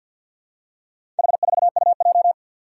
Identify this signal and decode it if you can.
Morse code, heard as a keyed tone, about 46 wpm, 700 Hz. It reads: H4UJ
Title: Four Challenging Letter-Number Combinations 50wpm